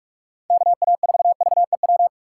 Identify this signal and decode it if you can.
Morse code, heard as a keyed tone, about 43 wpm, 700 Hz. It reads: KA4VEW
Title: USA Call Signs - 6 Characters - 6x Character Spacing - 45wpm